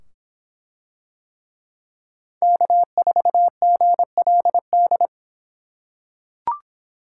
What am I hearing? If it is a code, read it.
Morse code, heard as a keyed tone, about 24 wpm, 700 Hz. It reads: K4GLD